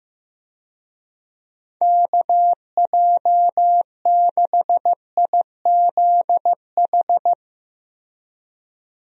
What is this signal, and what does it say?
Morse code, heard as a keyed tone, about 15 wpm, 700 Hz. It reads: KJ6IZH
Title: USA Call Signs - 6 Characters - Repeated 2x - 15wpm